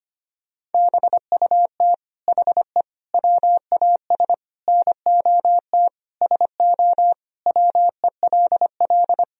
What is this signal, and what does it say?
Morse code, heard as a keyed tone, about 25 wpm, 700 Hz. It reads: BUT HE WAS NOT SO WELL
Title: Sentences from Top 1000 Words - Review Part 3 - 25wpm